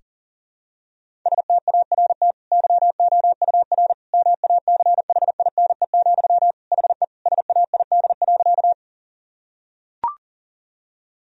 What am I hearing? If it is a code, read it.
Morse code, heard as a keyed tone, about 40 wpm, 700 Hz. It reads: START YOUR MACHINE, HE SAID.
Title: Sentences from Top 1000 Words - Review Part 2 - 40wpm